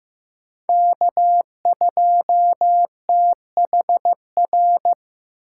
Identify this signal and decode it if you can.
Morse code, heard as a keyed tone, about 15 wpm, 700 Hz. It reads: K2THR
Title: USA Call Signs - 8x Character Spacing - 15wpm